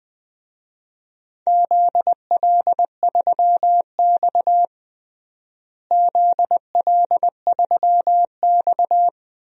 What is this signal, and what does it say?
Morse code, heard as a keyed tone, about 20 wpm, 700 Hz. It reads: ZL3X ZL3X
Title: DXCC-100 Call Signs - 4 Characters - Repeated 2x - 20wpm